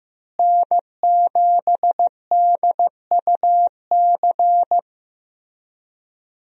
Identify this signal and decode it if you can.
Morse code, heard as a keyed tone, about 15 wpm, 700 Hz. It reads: N7DUC
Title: USA Call Signs - 5 Characters - Repeated 3x - 15wpm